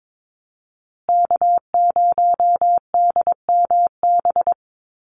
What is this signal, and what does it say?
Morse code, heard as a keyed tone, about 22 wpm, 700 Hz. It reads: K0DMB
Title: USA Call Signs Repeated 2x 22wpm